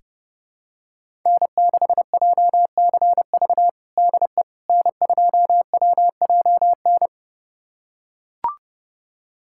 Morse code, heard as a keyed tone, about 32 wpm, 700 Hz. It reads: N6JCV DE N2WJN
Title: USA Callsign DE Callsign - 5 Characters - 30wpm